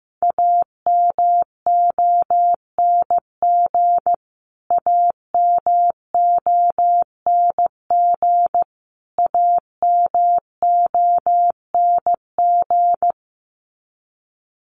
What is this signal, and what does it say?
Morse code, heard as a keyed tone, about 15 wpm, 700 Hz. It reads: AMONG AMONG AMONG
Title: Top 500 Words Repeated 15wpm